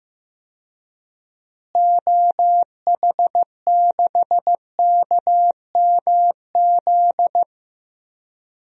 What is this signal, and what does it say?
Morse code, heard as a keyed tone, about 15 wpm, 700 Hz. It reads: OH6KMZ
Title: DXCC-100 Call Signs - 6 Characters - Repeated 3x - 15wpm